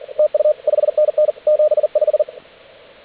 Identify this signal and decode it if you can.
Morse code, heard as a keyed tone, about 44 wpm, 610 Hz. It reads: TU 5NN 75
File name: AUDIO2019 / HG8DX CW